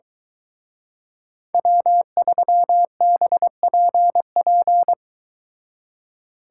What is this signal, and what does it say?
Morse code, heard as a keyed tone, about 23 wpm, 700 Hz. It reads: W3BPP